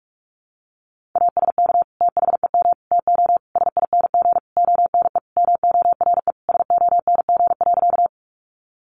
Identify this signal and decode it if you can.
Morse code, heard as a keyed tone, about 44 wpm, 700 Hz. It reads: ASK THEM TO SING ONE MORE SONG.
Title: Sentences from Top 400 Words 45wpm